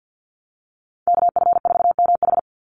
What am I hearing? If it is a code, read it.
Morse code, heard as a keyed tone, about 50 wpm, 700 Hz. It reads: KF4MH
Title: NA SOTA Chaser Callsigns 50wpm